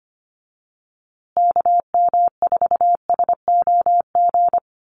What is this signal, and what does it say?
Morse code, heard as a keyed tone, about 25 wpm, 700 Hz. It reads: KM4SOG
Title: USA Call Signs Repeated 3x 25wpm